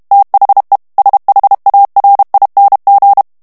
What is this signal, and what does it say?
Morse code, heard as a keyed tone, about 32 wpm, 800 Hz. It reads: THE SHARING
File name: morse mp3 / ANGLAIS texte clair 3 vitesse 30 mots minute